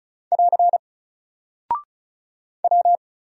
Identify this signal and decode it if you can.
Morse code, heard as a keyed tone, about 35 wpm, 700 Hz. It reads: <AR> W
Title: Single Character (Letter:Number:Punctuation:Prosign) 35wpm